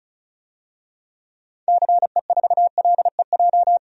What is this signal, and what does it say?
Morse code, heard as a keyed tone, about 34 wpm, 700 Hz. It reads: CE4LEJ
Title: DXCC-100 Call Signs - 6 Characters - 35wpm